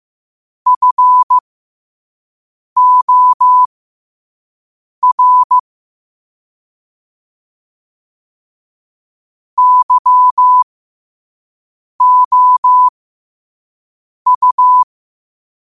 Morse code, heard as a keyed tone, about 15 wpm, spaced as for 5 wpm, 1000 Hz. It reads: FOR YOU
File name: qotdfiles / 5wpm